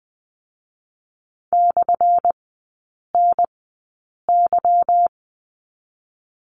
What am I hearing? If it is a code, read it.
Morse code, heard as a keyed tone, about 20 wpm, spaced as for 8 wpm, 700 Hz. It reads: /NY
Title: 205 Lesson 33e - Letter j - Three Characters 20wpm